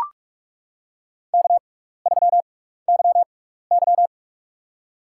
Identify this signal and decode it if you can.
Morse code, heard as a keyed tone, about 44 wpm, spaced as for 15 wpm, 700 Hz. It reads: K3YY
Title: USA Call Signs - 4 Characters - 6x Character Spacing - 45wpm